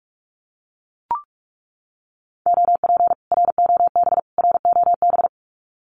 Morse code, heard as a keyed tone, about 44 wpm, 700 Hz. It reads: OP ROB ROB
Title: Common QSO Phrases Encore 45wpm